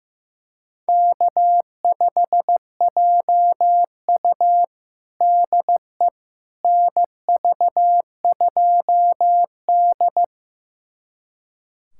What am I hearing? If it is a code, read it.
Morse code, heard as a keyed tone, about 15 wpm, 700 Hz. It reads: K5JU DE NV2D
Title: USA Callsign DE Callsign - 4 Characters - 15wpm